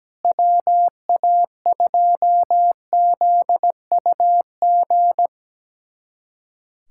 Morse code, heard as a keyed tone, about 17 wpm, 700 Hz. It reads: WA2ZUG
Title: USA Callsign DE Callsign - 6 Characters - 17wpm